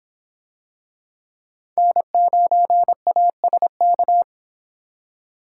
Morse code, heard as a keyed tone, about 24 wpm, 700 Hz. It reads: N9ASK